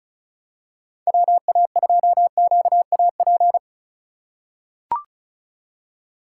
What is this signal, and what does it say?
Morse code, heard as a keyed tone, about 36 wpm, 700 Hz. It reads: WA2QAP
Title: USA Call Signs - 6 Characters - 6x Character Spacing - 35wpm